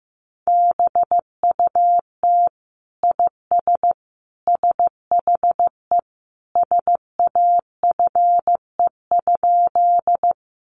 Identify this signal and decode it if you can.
Morse code, heard as a keyed tone, about 14 wpm, 700 Hz. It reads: BUT IS SHE SAFE?
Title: Sentences from Top 900 Words 15wpm